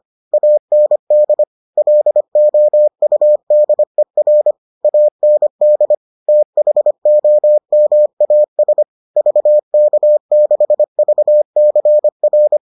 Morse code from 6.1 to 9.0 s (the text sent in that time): THOMAS